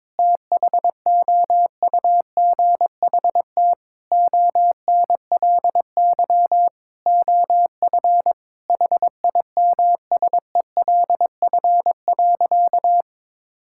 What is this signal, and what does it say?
Morse code, heard as a keyed tone, about 22 wpm, 700 Hz. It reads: THOUGHT ONLY OF HIMSELF.
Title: Sentences from Top 1600 Words 22wpm